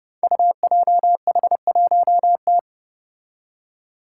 Morse code, heard as a keyed tone, about 30 wpm, 700 Hz. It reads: UJH1T
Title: Five Challenging Letter-Number Combinations 30wpm